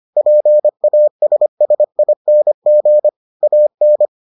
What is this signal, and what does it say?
Morse code, heard as a keyed tone, about 25 wpm, 600 Hz. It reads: PASSING AN